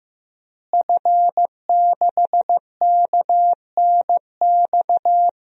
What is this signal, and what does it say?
Morse code, heard as a keyed tone, about 15 wpm, 700 Hz. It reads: F6KNX
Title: DXCC-100 Call Signs - 5 Characters - Repeated 3x - 15wpm